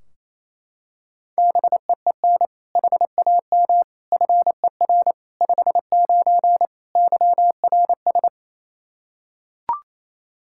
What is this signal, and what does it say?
Morse code, heard as a keyed tone, about 28 wpm, 700 Hz. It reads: BEEN HAM FER 59 YRS
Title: Practice QSO - Exchange 3 - More Info - 28wpm